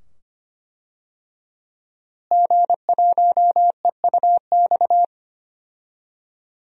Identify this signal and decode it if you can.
Morse code, heard as a keyed tone, about 25 wpm, 700 Hz. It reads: G1EUX
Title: DXCC-100 Call Signs - 5 Characters - Repeated 2x - 25wpm